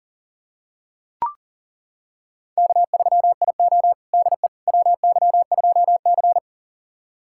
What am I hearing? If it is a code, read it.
Morse code, heard as a keyed tone, about 40 wpm, 700 Hz. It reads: K3IO DE WY2C
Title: USA Callsign DE Callsign - 4 Characters - 40wpm